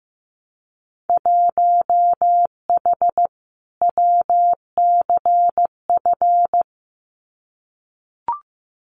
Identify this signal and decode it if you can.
Morse code, heard as a keyed tone, about 15 wpm, 700 Hz. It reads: 1H WCF
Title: Winter Field Day Exchanges Encore 15wpm